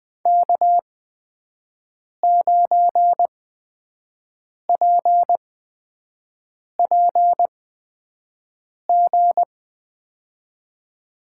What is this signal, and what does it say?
Morse code, heard as a keyed tone, about 20 wpm, spaced as for 5 wpm, 700 Hz. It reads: K9PPG
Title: USA Call Signs - 8x Character Spacing - 20wpm